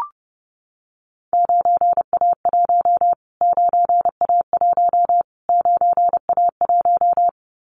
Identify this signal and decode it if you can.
Morse code, heard as a keyed tone, about 30 wpm, 700 Hz. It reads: 9A1 9A1 9A1
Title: Callsign Prefixes for Top 25 DXCC - Repeated 3x - 30wpm